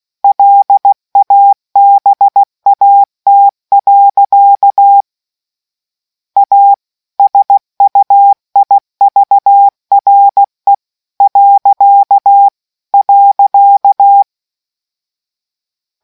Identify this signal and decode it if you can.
Morse code, heard as a keyed tone, about 16 wpm, 800 Hz. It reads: LABAT. A SUIVRE .